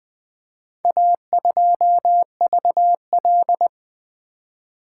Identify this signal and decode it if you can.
Morse code, heard as a keyed tone, about 20 wpm, 700 Hz. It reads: A2VL
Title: Four Letter-Number Combinations - 20wpm